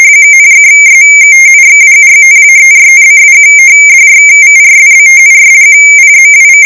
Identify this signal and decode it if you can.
Two-tone teleprinter signal, about 170 Hz shift.
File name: rtty / Afsk22msPerBit